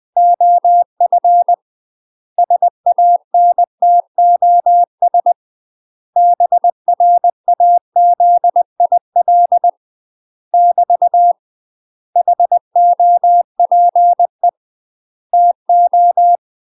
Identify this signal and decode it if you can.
Morse code, heard as a keyed tone, about 20 wpm, 700 Hz. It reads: OF SANTOS BRAZIL <BT> HOPE TO
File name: UK MASTER QSOs TRAINING / 20WPM / UK MASTER QSOs 0036